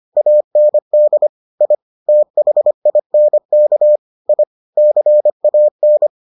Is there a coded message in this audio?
Yes: Morse code from 0.2 to 6.1 s, heard as a keyed tone, about 25 wpm, 600 Hz. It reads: AND I THINK I CAN